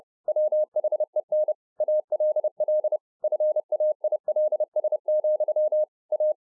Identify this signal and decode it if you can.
Morse code, heard as a keyed tone, about 30 wpm, 610 Hz. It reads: WHEN ALL FAILS, A